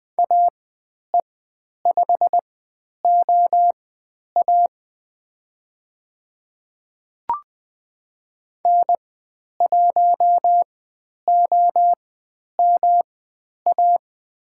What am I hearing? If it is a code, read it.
Morse code, heard as a keyed tone, about 20 wpm, spaced as for 10 wpm, 700 Hz. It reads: AE5OA N1OMA
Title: 109 Lesson 18g - Letter m - Call Signs 20wpm